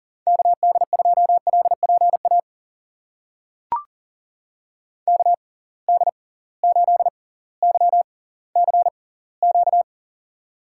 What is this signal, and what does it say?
Morse code, heard as a keyed tone, about 40 wpm, spaced as for 20 wpm, 700 Hz. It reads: KD2LPA KD8YCQ
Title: USA Call Signs - 6 Characters - 6x Character Spacing - 40wpm